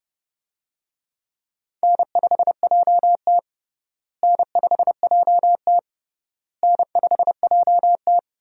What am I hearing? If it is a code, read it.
Morse code, heard as a keyed tone, about 32 wpm, 700 Hz. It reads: N5JT N5JT N5JT
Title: USA Call Signs - 4 Characters - Repeated 3x - 30wpm